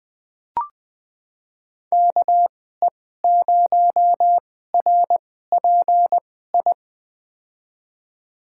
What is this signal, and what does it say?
Morse code, heard as a keyed tone, about 20 wpm, spaced as for 14 wpm, 700 Hz. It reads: KE0RPI
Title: USA Call Signs - 6 Characters - 2x Character Spacing - 20wpm